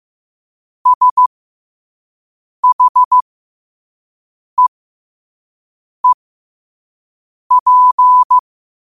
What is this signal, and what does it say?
Morse code, heard as a keyed tone, about 15 wpm, spaced as for 5 wpm, 1000 Hz. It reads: SHEEP
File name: qotdfiles / 5wpm